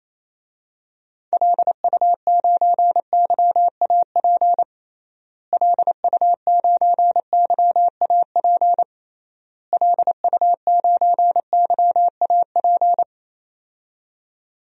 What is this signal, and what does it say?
Morse code, heard as a keyed tone, about 27 wpm, 700 Hz. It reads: LU9YAP LU9YAP LU9YAP
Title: DXCC-100 Call Signs - 6 Characters - Repeated 3x - 28wpm